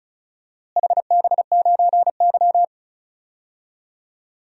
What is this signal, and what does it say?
Morse code, heard as a keyed tone, about 34 wpm, 700 Hz. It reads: HB9Y
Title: DXCC-100 Call Signs - 4 Characters - Repeated 3x - 35wpm